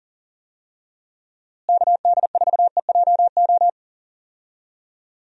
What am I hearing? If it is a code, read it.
Morse code, heard as a keyed tone, about 37 wpm, 700 Hz. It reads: KD4EJO
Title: USA Call Signs - 6 Characters - Repeated 3x - 40wpm